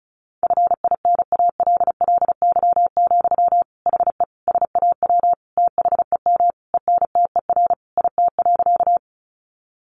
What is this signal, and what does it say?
Morse code, heard as a keyed tone, about 34 wpm, 700 Hz. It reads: FINALLY, HE SAW THEM ENTER IT.
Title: Sentences from Top 1300 Words 35wpm